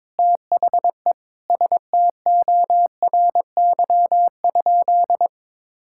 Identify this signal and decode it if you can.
Morse code, heard as a keyed tone, about 22 wpm, 700 Hz. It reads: THE STORY?
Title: Sentences from Top 1000 Words - Review Part 5 - 22wpm